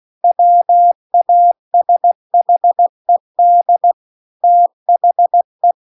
Morse code, heard as a keyed tone, about 16 wpm, 700 Hz. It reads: WASHED THE
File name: CW-The-Big-Twoheaded-River-2-16WPM